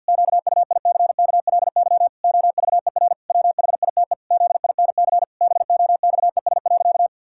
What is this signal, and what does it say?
Morse code, heard as a keyed tone, about 51 wpm, 700 Hz. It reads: QUICKLY OVER WHITE ZINC BOXES.
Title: Pangram Challenge v2 - 50wpm